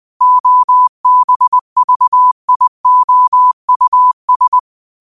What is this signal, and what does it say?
Morse code, heard as a keyed tone, about 20 wpm, 1000 Hz. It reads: OBVIOUS